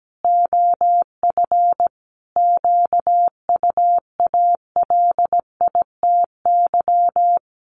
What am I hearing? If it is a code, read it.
Morse code, heard as a keyed tone, about 17 wpm, 700 Hz. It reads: OF QUALITY
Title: Sentences from Top 1500 Words 17wpm